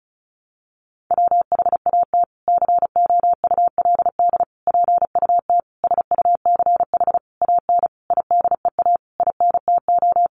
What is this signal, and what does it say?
Morse code, heard as a keyed tone, about 34 wpm, 700 Hz. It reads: WHAT COULD PUT SUCH AN IDEA INTO
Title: Sentences from Top 600 Words 35wpm